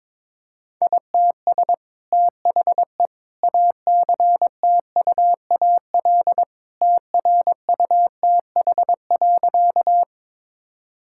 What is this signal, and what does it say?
Morse code, heard as a keyed tone, about 22 wpm, 700 Hz. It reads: ITS THE ACTUAL TRUTH.
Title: Sentences from Top 1000 Words - Review Part 5 - 22wpm